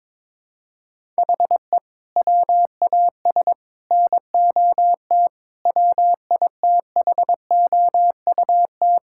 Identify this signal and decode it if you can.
Morse code, heard as a keyed tone, about 22 wpm, 700 Hz. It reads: HE WAS NOT WITHOUT
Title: Sentences from Top 1000 Words - Review Part 6 - 22wpm